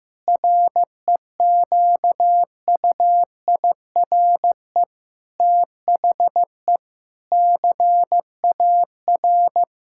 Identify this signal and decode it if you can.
Morse code, heard as a keyed tone, about 15 wpm, 700 Hz. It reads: REQUIRE THE CAR
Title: Sentences from Top 1000 Words - Review Part 6 - 15wpm